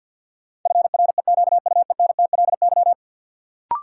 Morse code, heard as a keyed tone, about 53 wpm, 700 Hz. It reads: FREQUENTLY
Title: Sentences from Top 1300 Words - New Word Drill Alt - 50wpm